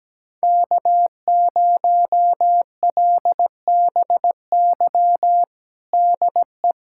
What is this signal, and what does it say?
Morse code, heard as a keyed tone, about 17 wpm, 700 Hz. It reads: K0LBY DE
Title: USA Callsign DE Callsign - 5 Characters - 17wpm